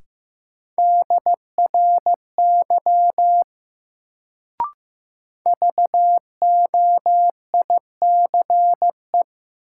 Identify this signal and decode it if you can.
Morse code, heard as a keyed tone, about 15 wpm, 700 Hz. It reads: DRY VOICE
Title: Sentences from Top 500 Words - New Word Drill - 15wpm